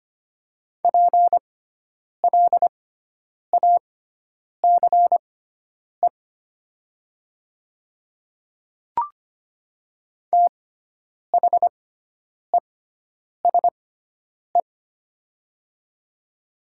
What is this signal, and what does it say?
Morse code, heard as a keyed tone, about 25 wpm, spaced as for 9 wpm, 700 Hz. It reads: PLACE THESE
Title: 6x Character Spacing - Top 300 Words - 25wpm